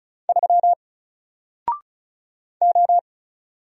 Morse code, heard as a keyed tone, about 36 wpm, 700 Hz. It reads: 3 O
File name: Single-Letter-Number-35wpm